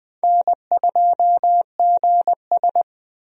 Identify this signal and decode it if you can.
Morse code, heard as a keyed tone, about 20 wpm, 700 Hz. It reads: N2GS
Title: USA Call Signs - 4 Characters - 20wpm